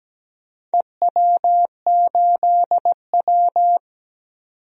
Morse code, heard as a keyed tone, about 17 wpm, 700 Hz. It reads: EW8W
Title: DXCC-100 Call Signs - 4 Characters - 17wpm